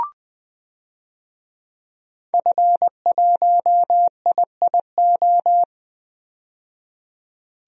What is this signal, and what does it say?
Morse code, heard as a keyed tone, about 20 wpm, 700 Hz. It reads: F1IIO
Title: DXCC-100 Call Signs - 5 Characters - 20wpm